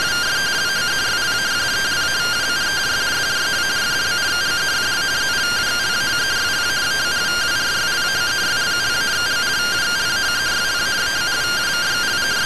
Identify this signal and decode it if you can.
Two-tone teleprinter signal, about 170 Hz shift.